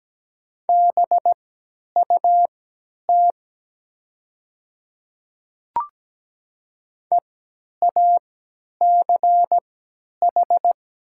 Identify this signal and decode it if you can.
Morse code, heard as a keyed tone, about 17 wpm, spaced as for 10 wpm, 700 Hz. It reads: BUT EACH
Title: 3x Character Spacing - Top 200 Words - 17wpm